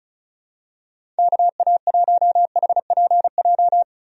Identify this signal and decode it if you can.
Morse code, heard as a keyed tone, about 36 wpm, 700 Hz. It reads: KA1HPJ
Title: USA Call Signs - 6 Characters - 35wpm